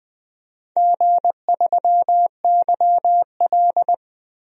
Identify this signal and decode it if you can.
Morse code, heard as a keyed tone, about 20 wpm, 700 Hz. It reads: G3YL
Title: DXCC-100 Call Signs - 4 Characters - 20wpm